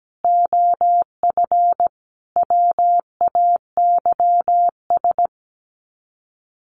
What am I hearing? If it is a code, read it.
Morse code, heard as a keyed tone, about 17 wpm, 700 Hz. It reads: OF WAYS
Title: Sets of 5 Words Encore 17wpm